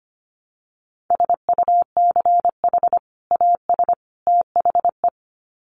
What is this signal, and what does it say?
Morse code, heard as a keyed tone, about 25 wpm, 700 Hz. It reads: SUCH AS THE
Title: Sets of 3 Words 25wpm